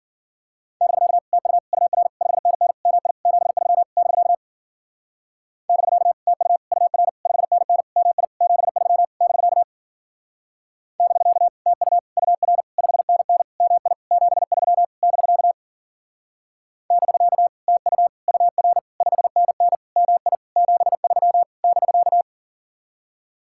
Morse code, heard as a keyed tone, about 48 wpm, 700 Hz. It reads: <BK> TU UR 5NN MI 73 <BK> <BK> TU UR 5NN MI 73 <BK> <BK> TU UR 5NN MI 73 <BK> <BK> TU UR 5NN MI 73 <BK>
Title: Speed-Racing Common QSO Phrases - 40wpm